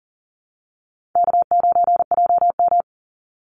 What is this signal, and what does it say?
Morse code, heard as a keyed tone, about 40 wpm, 700 Hz. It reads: K9JM
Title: Easy Call Signs - 40wpm